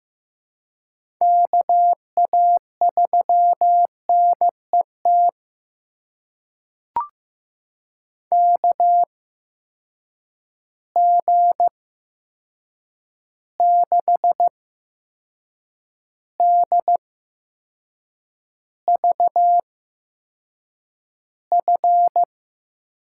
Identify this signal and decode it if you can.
Morse code, heard as a keyed tone, about 15 wpm, spaced as for 6 wpm, 700 Hz. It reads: KA3NET KG6DVF
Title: USA Call Signs - 6 Characters - 8x Character Spacing - 15wpm